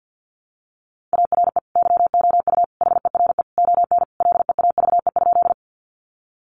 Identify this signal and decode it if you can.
Morse code, heard as a keyed tone, about 50 wpm, 700 Hz. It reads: ARE YOU HERE ON LEAVE?
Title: Sentences from Top 400 Words 50wpm